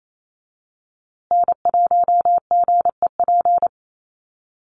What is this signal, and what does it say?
Morse code, heard as a keyed tone, about 28 wpm, 700 Hz. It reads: N1GEP
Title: USA Call Signs Repeated 3x 28wpm